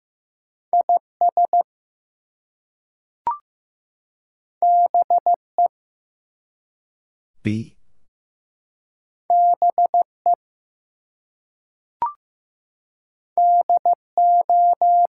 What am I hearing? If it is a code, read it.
Morse code, heard as a keyed tone, about 15 wpm, 700 Hz. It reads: IS BE BE DO
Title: Two Letter Words - 15wpm